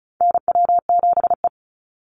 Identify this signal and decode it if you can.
Morse code, heard as a keyed tone, about 34 wpm, 700 Hz. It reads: NW7E
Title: NA SOTA Chaser Callsigns 35wpm